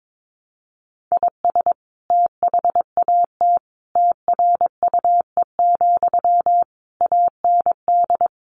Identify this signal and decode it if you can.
Morse code, heard as a keyed tone, about 22 wpm, 700 Hz. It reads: IS THAT TRUE, AND
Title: Sentences from Top 1200 Words 22wpm